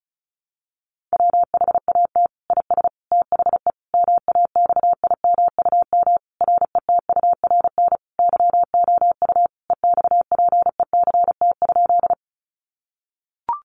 Morse code, heard as a keyed tone, about 34 wpm, 700 Hz. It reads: WHAT IS THE MAXIMUM RETURN YOU EXPECT?
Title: Sentences from Top 1300 Words 35wpm